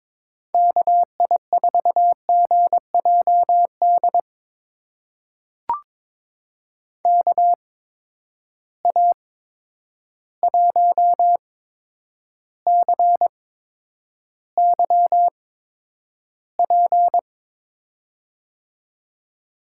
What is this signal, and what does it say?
Morse code, heard as a keyed tone, about 22 wpm, spaced as for 9 wpm, 700 Hz. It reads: KI4GJD KA1CYP
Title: USA Call Signs - 6 Characters - 8x Character Spacing - 22wpm